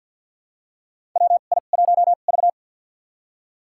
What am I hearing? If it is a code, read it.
Morse code, heard as a keyed tone, about 50 wpm, spaced as for 36 wpm, 700 Hz. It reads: WI1V
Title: USA Call Signs - 2x Character Spacing - 50wpm